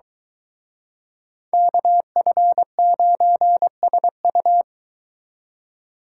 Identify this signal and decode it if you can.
Morse code, heard as a keyed tone, about 24 wpm, 700 Hz. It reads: KF9SU